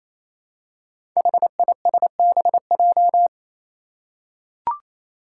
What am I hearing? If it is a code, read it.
Morse code, heard as a keyed tone, about 28 wpm, 700 Hz. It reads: HISBJ
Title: Five Challenging Letter-Number Combinations 28wpm